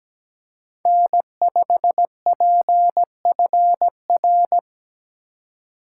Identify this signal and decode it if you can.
Morse code, heard as a keyed tone, about 17 wpm, 700 Hz. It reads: N5PFR
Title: USA Call Signs - 2x Character Spacing - 17wpm